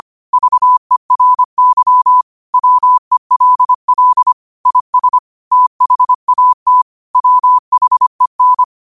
Morse code, heard as a keyed tone, about 25 wpm, 1000 Hz. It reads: VERY WELL IS THAT WHEN
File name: qotdfiles / 25wpm